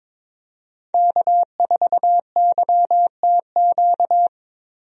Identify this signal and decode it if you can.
Morse code, heard as a keyed tone, about 22 wpm, 700 Hz. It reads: K4YTQ
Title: USA Call Signs - 5 Characters - Repeated 3x - 22wpm